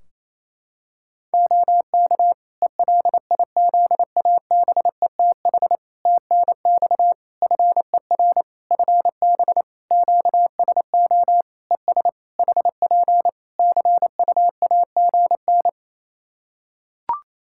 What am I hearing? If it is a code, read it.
Morse code, heard as a keyed tone, about 28 wpm, 700 Hz. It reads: OK ELIZABETH TNX FER FB QSO ES HP CUAGN
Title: Practice QSO - Exchange 4 - Ending - 28wpm